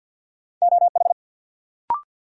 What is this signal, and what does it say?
Morse code, heard as a keyed tone, about 52 wpm, 700 Hz. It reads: OH
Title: Sentences from Top 500 Words - New Word Drill - 50wpm